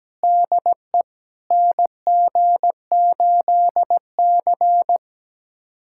Morse code, heard as a keyed tone, about 17 wpm, 700 Hz. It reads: DE NG8C
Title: USA Callsign DE Callsign - 4 Characters - 17wpm